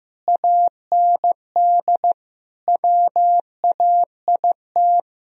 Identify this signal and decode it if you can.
Morse code, heard as a keyed tone, about 15 wpm, 700 Hz. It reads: AND WAIT
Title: Sentences from Top 1000 Words - Review Part 1 - 15wpm